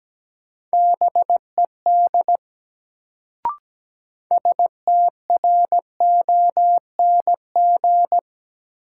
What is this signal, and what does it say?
Morse code, heard as a keyed tone, about 17 wpm, 700 Hz. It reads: BED STRONG
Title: Sentences from Top 500 Words - New Word Drill - 17wpm